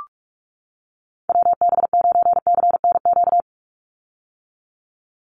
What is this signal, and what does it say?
Morse code, heard as a keyed tone, about 46 wpm, 700 Hz. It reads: WB9CNQ
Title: USA Call Signs 45wpm